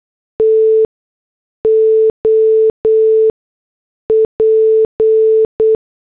Morse code, heard as a keyed tone, about 8 wpm, spaced as for 6 wpm, 440 Hz. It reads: TOP
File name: morse / headlines